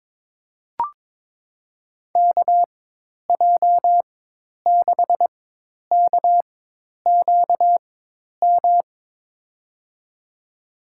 Morse code, heard as a keyed tone, about 22 wpm, spaced as for 10 wpm, 700 Hz. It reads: KJ6KQM
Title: USA Call Signs - 6 Characters - 4x Character Spacing - 22wpm